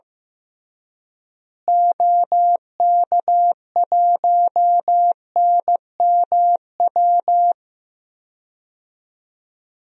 Morse code, heard as a keyed tone, about 15 wpm, 700 Hz. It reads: OK1NMW
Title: DXCC-100 Call Signs - 6 Characters - Repeated 3x - 15wpm